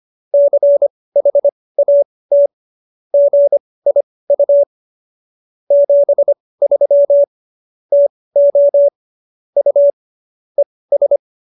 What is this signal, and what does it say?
Morse code, heard as a keyed tone, about 25 wpm, spaced as for 18 wpm, 570 Hz. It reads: CHAT GIU 73 TO U ES